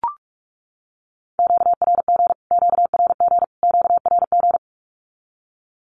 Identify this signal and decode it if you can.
Morse code, heard as a keyed tone, about 44 wpm, 700 Hz. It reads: QRG QRG QRG
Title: Q-Codes Repeated 3x 45wpm